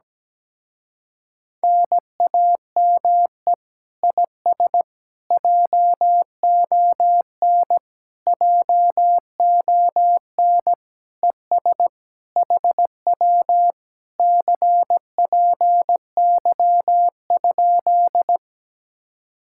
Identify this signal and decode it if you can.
Morse code, heard as a keyed tone, about 17 wpm, 700 Hz. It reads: NAME IS JON JON ES HW CPY?
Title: Speed-Racing Common QSO Phrases Encore - 17wpm